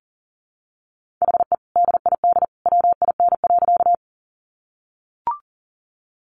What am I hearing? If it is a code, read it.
Morse code, heard as a keyed tone, about 40 wpm, 700 Hz. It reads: HE DID WIN.
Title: Sentences from Top 1500 Words 40wpm